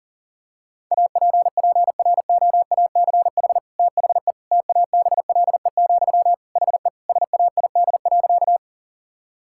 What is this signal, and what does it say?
Morse code, heard as a keyed tone, about 37 wpm, 690 Hz. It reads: APPROACH THE TABLE, HE SAID.
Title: Sentences from Top 1000 Words - Review Part 6 - 40wpm